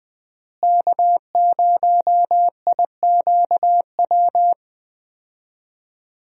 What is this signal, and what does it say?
Morse code, heard as a keyed tone, about 20 wpm, 700 Hz. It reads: K0IQW
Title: USA Call Signs - 5 Characters - Repeated 2x - 20wpm